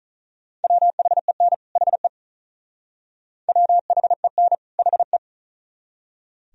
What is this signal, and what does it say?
Morse code, heard as a keyed tone, about 44 wpm, 700 Hz. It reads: WHEN HE WHEN HE